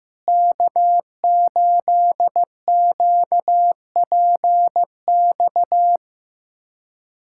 Morse code, heard as a keyed tone, about 16 wpm, 700 Hz. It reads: K8QPX
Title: USA Callsign DE Callsign - 5 Characters - 15wpm